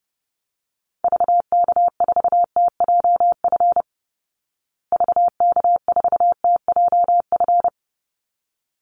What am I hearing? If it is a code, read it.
Morse code, heard as a keyed tone, about 30 wpm, 700 Hz. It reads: VK4TJF VK4TJF
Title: Call Signs Repeated 3x - 30wpm